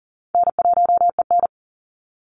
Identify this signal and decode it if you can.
Morse code, heard as a keyed tone, about 40 wpm, 700 Hz. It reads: N1EN
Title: Call Signs - 40wpm